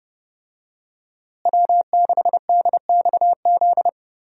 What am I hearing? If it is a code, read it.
Morse code, heard as a keyed tone, about 30 wpm, 700 Hz. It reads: W6DXZ
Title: USA Call Signs - 5 Characters - Repeated 2x - 30wpm